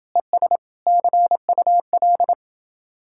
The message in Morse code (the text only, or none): ES CUL